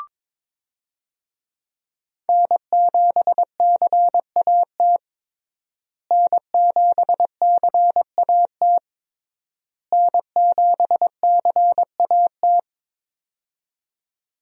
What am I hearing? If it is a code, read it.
Morse code, heard as a keyed tone, about 22 wpm, 700 Hz. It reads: N7CAT N7CAT N7CAT
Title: USA Call Signs - 5 Characters - Repeated 3x - 22wpm